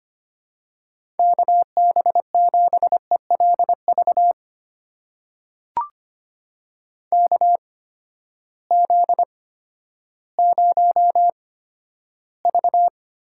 Morse code, heard as a keyed tone, about 25 wpm, spaced as for 12 wpm, 700 Hz. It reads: KB7ELV KZ0V
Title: USA Call Signs - 8x Character Spacing - 25wpm